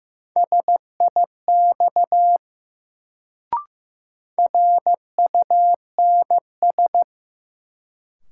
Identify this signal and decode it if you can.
Morse code, heard as a keyed tone, about 15 wpm, 700 Hz. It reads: SIX RUNS
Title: Sentences from Top 600 Words - New Word Drill - 15wpm